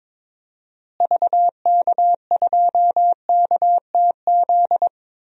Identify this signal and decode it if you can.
Morse code, heard as a keyed tone, about 21 wpm, 700 Hz. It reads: VK2KTZ
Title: DXCC-100 Call Signs - 6 Characters - Repeated 2x - 22wpm